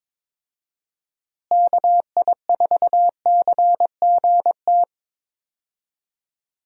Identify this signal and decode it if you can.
Morse code, heard as a keyed tone, about 22 wpm, 700 Hz. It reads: KI4CGT
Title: USA Call Signs - 6 Characters - Repeated 2x - 22wpm